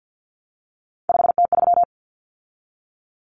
Morse code, heard as a keyed tone, about 50 wpm, 700 Hz. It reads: 5T3
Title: Common Cut Numbers - 3-Digit 50wpm